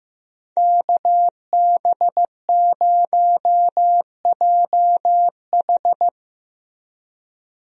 Morse code, heard as a keyed tone, about 15 wpm, 700 Hz. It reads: KB0JH
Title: USA Call Signs - 5 Characters - Repeated 3x - 15wpm